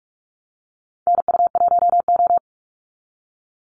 Morse code, heard as a keyed tone, about 44 wpm, 700 Hz. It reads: NU1O
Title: Call Signs - 45wpm